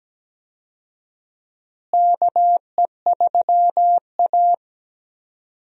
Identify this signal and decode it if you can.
Morse code, heard as a keyed tone, about 17 wpm, 700 Hz. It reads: KE3A
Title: USA Call Signs - 4 Characters - Repeated 2x - 17wpm